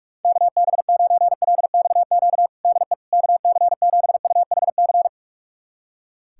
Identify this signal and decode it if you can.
Morse code, heard as a keyed tone, about 41 wpm, 700 Hz. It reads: KB9LXQ DE KC7UHC
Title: USA Callsign DE Callsign - 6 Characters - 45wpm